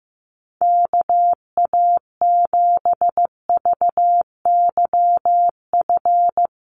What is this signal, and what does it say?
Morse code, heard as a keyed tone, about 15 wpm, 700 Hz. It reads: KA7VYF
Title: USA Call Signs Repeated 3x 15wpm